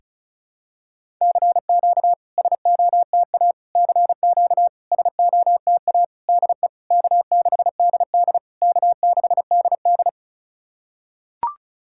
Morse code, heard as a keyed tone, about 36 wpm, 700 Hz. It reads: CQ SOTA CQ SOTA DE K6DD K6DD
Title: SOTA QSO Practice 35wpm